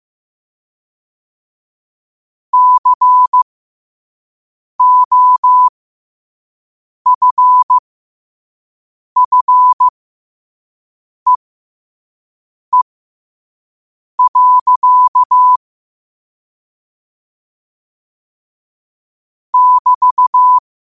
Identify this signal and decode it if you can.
Morse code, heard as a keyed tone, about 15 wpm, spaced as for 5 wpm, 1000 Hz. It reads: COFFEE. <BT>
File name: qotdfiles / 5wpm